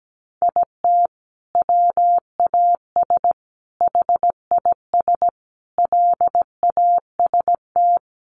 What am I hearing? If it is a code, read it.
Morse code, heard as a keyed tone, about 17 wpm, 700 Hz. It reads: IT WAS HIS LAST
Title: Sentences from Top 1200 Words 17wpm